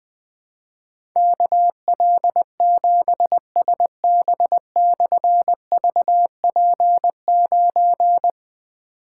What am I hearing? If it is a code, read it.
Morse code, heard as a keyed tone, about 20 wpm, 700 Hz. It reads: KL7SB/VP9
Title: Difficult Call Signs 20wpm